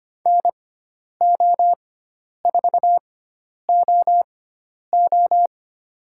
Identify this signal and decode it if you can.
Morse code, heard as a keyed tone, about 25 wpm, spaced as for 10 wpm, 700 Hz. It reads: NO4OO
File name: Lesson 16g - Letter c - Call Signs 25wpm